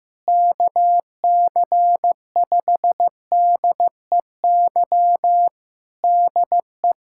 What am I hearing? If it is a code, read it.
Morse code, heard as a keyed tone, about 15 wpm, 700 Hz. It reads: KC5DEY DE
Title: Practice QSO - Exchange 4 - Ending - 15wpm